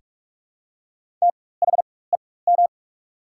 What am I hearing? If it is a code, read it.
Morse code, heard as a keyed tone, about 46 wpm, spaced as for 21 wpm, 700 Hz. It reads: THEM
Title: 4x Character Spacing - Top 500 Words - 45wpm